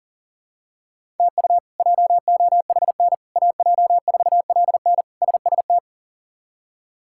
Morse code, heard as a keyed tone, about 40 wpm, 700 Hz. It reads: TU JOHN AJ4LN SST
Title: K1USN SST QSO Practice - 40wpm